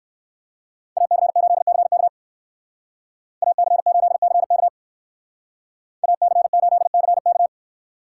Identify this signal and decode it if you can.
Morse code, heard as a keyed tone, about 56 wpm, 700 Hz. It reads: AC8CK AC8CK AC8CK
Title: Speed Racing - USA Call Signs 45wpm